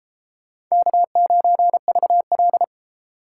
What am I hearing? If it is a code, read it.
Morse code, heard as a keyed tone, about 36 wpm, 700 Hz. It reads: K9VL